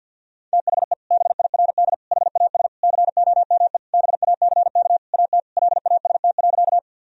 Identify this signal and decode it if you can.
Morse code, heard as a keyed tone, about 51 wpm, 700 Hz. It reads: THE BIRD HAS COME BACK AT LAST.
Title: Sentences from Top 1000 Words - Review Part 2 - 50wpm